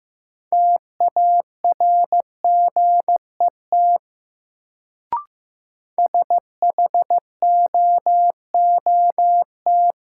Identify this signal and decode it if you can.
Morse code, heard as a keyed tone, about 15 wpm, 700 Hz. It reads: TARGET SHOOT
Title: Sentences from Top 1200 Words - New Word Drill - 15wpm